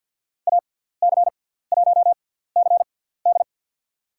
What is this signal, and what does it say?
Morse code, heard as a keyed tone, about 54 wpm, spaced as for 17 wpm, 700 Hz. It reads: AC1CD
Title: USA Call Signs - 5 Characters - 6x Character Spacing - 50wpm